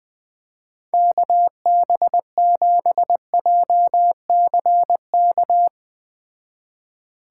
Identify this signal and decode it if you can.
Morse code, heard as a keyed tone, about 20 wpm, 700 Hz. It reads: KB7JCK